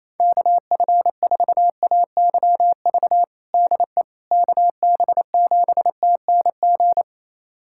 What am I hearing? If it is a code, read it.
Morse code, heard as a keyed tone, about 27 wpm, 700 Hz. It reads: KF4AYV DE KB7TNG
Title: USA Callsign DE Callsign - 6 Characters - 28wpm